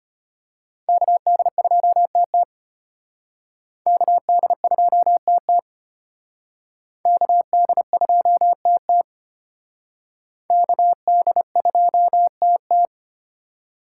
Morse code, heard as a keyed tone, about 32 wpm, 700 Hz. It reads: KD2TT KD2TT KD2TT KD2TT